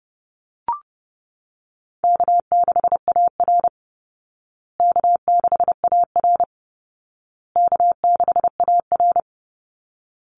Morse code, heard as a keyed tone, about 30 wpm, 700 Hz. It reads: K6AR K6AR K6AR
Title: Call Signs Repeated 2x - 30wpm